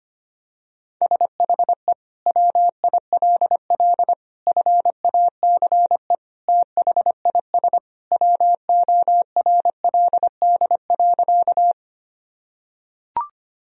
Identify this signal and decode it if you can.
Morse code, heard as a keyed tone, about 25 wpm, 700 Hz. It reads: SHE WILL FACE THIS WORLD.
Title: Sentences from Top 1000 Words - Review Part 2 - 25wpm